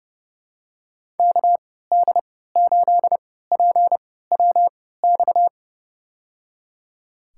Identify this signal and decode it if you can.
Morse code, heard as a keyed tone, about 28 wpm, spaced as for 17 wpm, 700 Hz. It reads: KD8PWX
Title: USA Call Signs - 3x Character Spacing - 30wpm